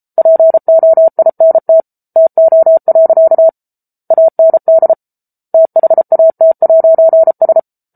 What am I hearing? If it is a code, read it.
Morse code, heard as a keyed tone, about 34 wpm, 650 Hz. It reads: POINT TO. AND THAT'S